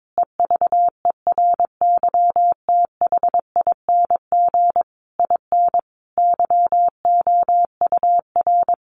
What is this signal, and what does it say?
Morse code, heard as a keyed tone, about 22 wpm, 700 Hz. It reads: EVERYTHING IN YOUR
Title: Sentences from Top 700 Words 22wpm